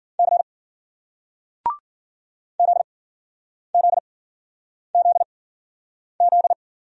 Morse code, heard as a keyed tone, about 50 wpm, spaced as for 9 wpm, 700 Hz. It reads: C ZZZZ